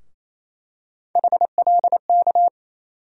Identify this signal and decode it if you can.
Morse code, heard as a keyed tone, about 28 wpm, 700 Hz. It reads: HLK
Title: USA Call Sign Postfixes - 3 Letters 28wpm